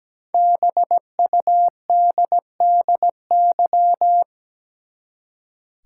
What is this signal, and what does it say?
Morse code, heard as a keyed tone, about 17 wpm, 700 Hz. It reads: BUDDY
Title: Five Letter Words - 17wpm